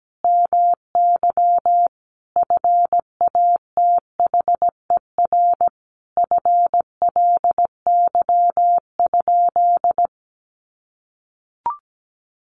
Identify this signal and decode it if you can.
Morse code, heard as a keyed tone, about 17 wpm, 700 Hz. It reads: MY FATHER FLY?
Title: Sentences from Top 500 Words 17wpm